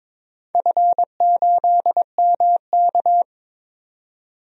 Morse code, heard as a keyed tone, about 22 wpm, 700 Hz. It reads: F8MK
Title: DXCC-100 Call Signs - 4 Characters - 22wpm